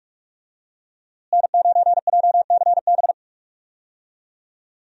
Morse code, heard as a keyed tone, about 46 wpm, 700 Hz. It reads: N9JCB
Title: USA Call Signs - 5 Characters - Repeated 3x - 45wpm